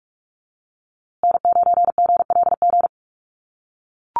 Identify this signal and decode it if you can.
Morse code, heard as a keyed tone, about 44 wpm, 700 Hz. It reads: N9GLG
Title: USA Call Signs Repeated 2x 45wpm